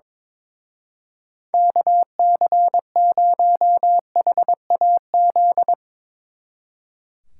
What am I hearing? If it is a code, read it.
Morse code, heard as a keyed tone, about 22 wpm, 700 Hz. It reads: KC0HAZ
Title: Speed Racing - USA 2x3 Call Signs 22wpm